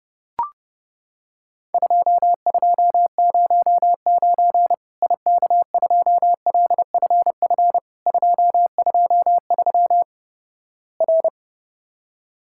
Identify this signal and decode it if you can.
Morse code, heard as a keyed tone, about 29 wpm, 700 Hz. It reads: 2209 IK2LFF 223 R
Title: Worked All Europe DX Contest - QTC Exchange - 30wpm